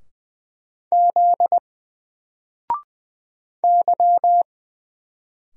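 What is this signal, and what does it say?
Morse code, heard as a keyed tone, about 20 wpm, 700 Hz. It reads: Z Y
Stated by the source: Single letter and number Morse drill at 20 wpm, from N4LCD.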